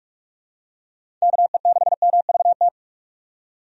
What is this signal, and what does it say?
Morse code, heard as a keyed tone, about 41 wpm, 700 Hz. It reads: KE6MVT
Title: USA Call Signs - 6 Characters - 45wpm